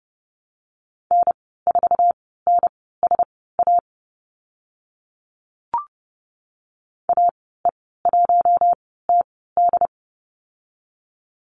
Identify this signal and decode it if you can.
Morse code, heard as a keyed tone, about 29 wpm, spaced as for 17 wpm, 700 Hz. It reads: N4NSA AE1TD
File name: Lesson 14g - Number 2 - Call Signs 30wpm